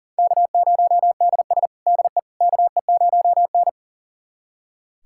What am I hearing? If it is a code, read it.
Morse code, heard as a keyed tone, about 40 wpm, 700 Hz. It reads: K0DS DE KE0N
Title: USA Callsign DE Callsign - 4 Characters - 40wpm